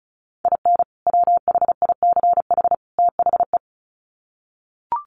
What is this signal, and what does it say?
Morse code, heard as a keyed tone, about 34 wpm, 700 Hz. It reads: IN WHICH THE
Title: Sets of 3 Words 35wpm